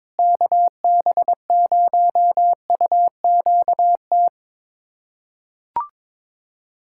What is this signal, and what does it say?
Morse code, heard as a keyed tone, about 22 wpm, 700 Hz. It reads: KB0UQT
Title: USA Callsign DE Callsign 22wpm